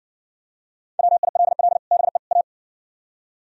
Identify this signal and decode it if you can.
Morse code, heard as a keyed tone, about 56 wpm, 700 Hz. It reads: WILL BE A